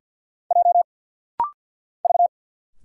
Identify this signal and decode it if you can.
Morse code, heard as a keyed tone, about 50 wpm, 700 Hz. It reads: J V
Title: Single Letters - Rapid-Fire 50wpm